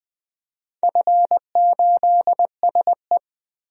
Morse code, heard as a keyed tone, about 20 wpm, 700 Hz. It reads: F8SE
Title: DXCC-100 Call Signs - 4 Characters - Repeated 2x - 20wpm